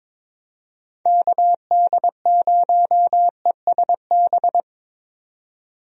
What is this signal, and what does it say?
Morse code, heard as a keyed tone, about 22 wpm, 700 Hz. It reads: KD0ESB
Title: USA Call Signs - 6 Characters - 22wpm